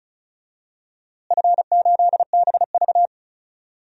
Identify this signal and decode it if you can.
Morse code, heard as a keyed tone, about 36 wpm, 700 Hz. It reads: F8BV
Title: DXCC-100 Call Signs - 4 Characters - Repeated 3x - 35wpm